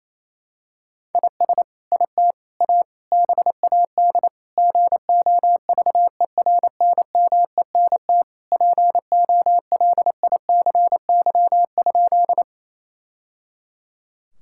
Morse code, heard as a keyed tone, about 28 wpm, 700 Hz. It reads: IS IT A BAD GOVERNMENT POLICY?
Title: Sentences from Top 1000 Words - Review Part 5 - 28wpm